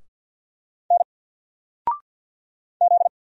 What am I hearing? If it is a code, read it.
Morse code, heard as a keyed tone, about 48 wpm, 700 Hz. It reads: N Z
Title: Single Letter-Number Alt 50wpm